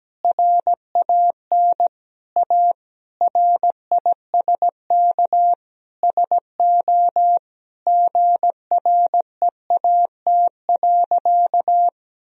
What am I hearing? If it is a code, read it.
Morse code, heard as a keyed tone, about 17 wpm, 700 Hz. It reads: RAN A RISK SO GREAT.
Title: Sentences from Top 1000 Words - Review Part 4 - 17wpm